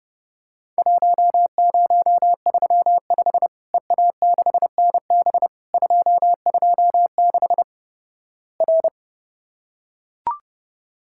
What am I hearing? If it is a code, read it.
Morse code, heard as a keyed tone, about 32 wpm, 700 Hz. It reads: 1035 EA6NB 226 R
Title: Worked All Europe DX Contest - QTC Exchange - 30wpm